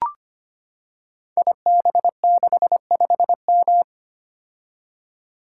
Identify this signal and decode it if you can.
Morse code, heard as a keyed tone, about 25 wpm, 700 Hz. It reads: IB65M
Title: Code Groups - 5 Letter:Numbers - 25wpm